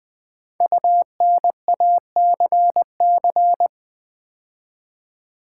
Morse code, heard as a keyed tone, about 20 wpm, 700 Hz. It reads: UNACC
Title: Build A Word - 5 Letters At A Time - 12-Letter Words - 20wpm